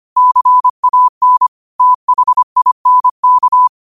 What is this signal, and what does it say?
Morse code, heard as a keyed tone, about 25 wpm, 1000 Hz. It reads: CAN THINK
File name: qotdfiles / 25wpm